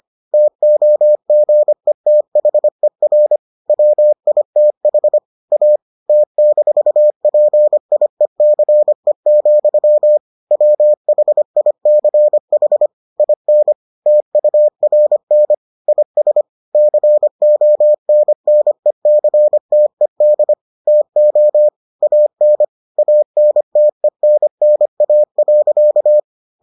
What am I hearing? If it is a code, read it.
Morse code, heard as a keyed tone, about 25 wpm, 600 Hz. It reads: TOGETHER WITH A T-PIECE, WHICH IN TURN IS CONNECTED TO AN ANTENNA.